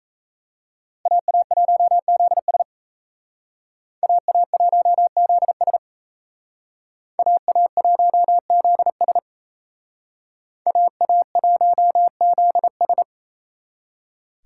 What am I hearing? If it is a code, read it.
Morse code, heard as a keyed tone, about 35 wpm, 700 Hz. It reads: AA1ZS AA1ZS AA1ZS AA1ZS